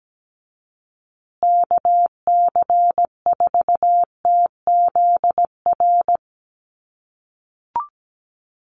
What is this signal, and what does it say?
Morse code, heard as a keyed tone, about 17 wpm, 700 Hz. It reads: KC4TZR
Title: USA Call Signs Repeated 3x 17wpm